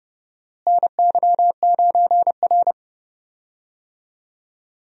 Morse code, heard as a keyed tone, about 30 wpm, 700 Hz. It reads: NY9R
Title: USA Call Signs - 4 Characters - Repeated 3x - 30wpm